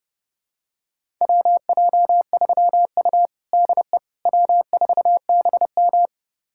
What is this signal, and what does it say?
Morse code, heard as a keyed tone, about 31 wpm, 700 Hz. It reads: WJ3U DE W4BM
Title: USA Callsign DE Callsign - 4 Characters - 30wpm